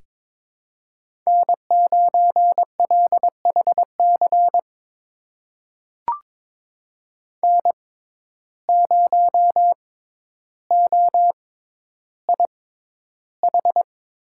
Morse code, heard as a keyed tone, about 22 wpm, spaced as for 11 wpm, 700 Hz. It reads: N9LHC N0OIH
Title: USA Call Signs - 5 Characters - 6x Character Spacing - 22wpm